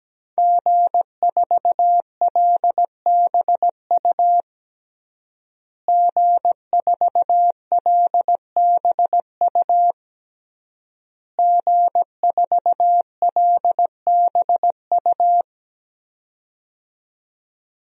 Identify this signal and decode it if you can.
Morse code, heard as a keyed tone, about 17 wpm, 700 Hz. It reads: G4LBU G4LBU G4LBU
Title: DXCC-100 Call Signs - 5 Characters - Repeated 3x - 17wpm